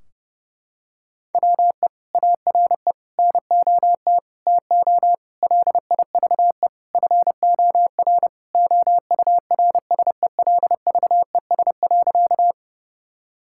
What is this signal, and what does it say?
Morse code, heard as a keyed tone, about 32 wpm, 700 Hz. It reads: WE ARE NOT TO LIVE FOR OURSELVES.
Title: Sentences from Top 1600 Words 30wpm